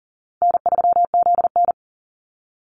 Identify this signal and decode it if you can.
Morse code, heard as a keyed tone, about 40 wpm, 700 Hz. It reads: N3ZN
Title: Call Signs Repeated 2x - 40wpm